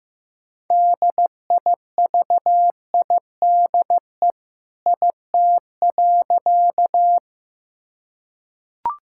Morse code, heard as a keyed tone, about 15 wpm, 700 Hz. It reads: DIVIDE IT.
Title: Sentences from Top 1600 Words 15wpm